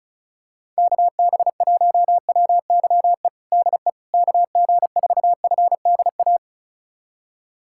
Morse code, heard as a keyed tone, about 34 wpm, 690 Hz. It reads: KB1WYE DE KG4FDA
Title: USA Callsign DE Callsign - 6 Characters - 35wpm